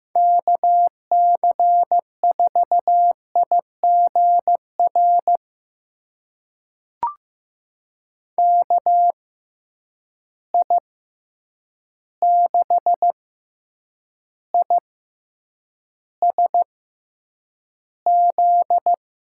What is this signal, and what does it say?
Morse code, heard as a keyed tone, about 15 wpm, spaced as for 8 wpm, 700 Hz. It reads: KC4IGR KI6ISZ
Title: USA Call Signs - 6 Characters - 6x Character Spacing - 15wpm